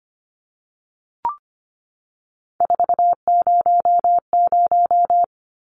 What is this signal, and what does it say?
Morse code, heard as a keyed tone, about 25 wpm, 700 Hz. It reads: 400
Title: Numbers - 3-Digit 25wpm